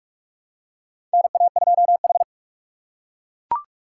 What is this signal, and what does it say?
Morse code, heard as a keyed tone, about 42 wpm, 700 Hz. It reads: NA2H
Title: USA Call Signs - 3x Character Spacing - 45wpm